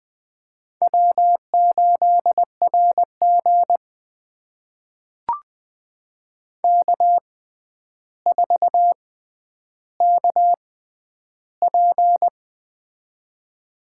Morse code, heard as a keyed tone, about 21 wpm, spaced as for 10 wpm, 700 Hz. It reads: W8RG K4KP
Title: USA Call Signs - 4 Characters - 6x Character Spacing - 20wpm